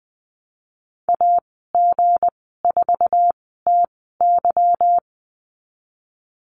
Morse code, heard as a keyed tone, about 20 wpm, spaced as for 15 wpm, 700 Hz. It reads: AG4TY
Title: 179 Lesson 29g - Slash - Call Signs 20wpm